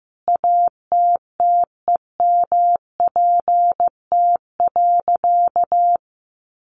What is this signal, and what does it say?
Morse code, heard as a keyed tone, about 15 wpm, 700 Hz. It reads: ATTEMPT.
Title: Sentences from Top 1000 Words 15wpm